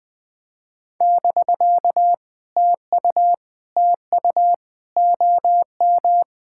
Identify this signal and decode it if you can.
Morse code, heard as a keyed tone, about 20 wpm, 700 Hz. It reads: <BK> TU TU OM
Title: POTA QSO Practice - 20wpm